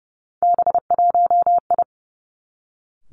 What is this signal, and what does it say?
Morse code, heard as a keyed tone, about 30 wpm, 700 Hz. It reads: B1I
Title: Three Challenging Letter-Number Combinations 30wpm